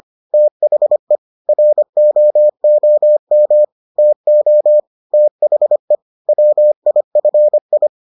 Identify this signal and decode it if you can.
Morse code, heard as a keyed tone, about 25 wpm, 600 Hz. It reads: THE ROOM TO THE WIFI